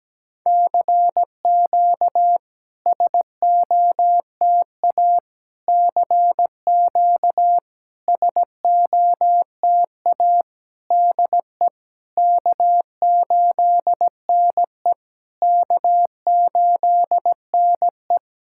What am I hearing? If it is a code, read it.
Morse code, heard as a keyed tone, about 17 wpm, 700 Hz. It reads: CQ SOTA CQ SOTA DE K8NE K8NE
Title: SOTA QSO Practice 17wpm